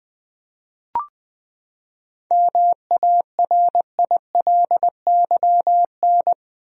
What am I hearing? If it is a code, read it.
Morse code, heard as a keyed tone, about 20 wpm, 700 Hz. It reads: MARILYN
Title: USA Common Names - 20wpm